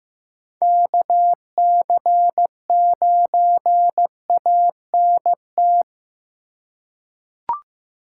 Morse code, heard as a keyed tone, about 15 wpm, 700 Hz. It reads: KC9ANT
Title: USA Call Signs - 6 Characters - 3x Character Spacing - 15wpm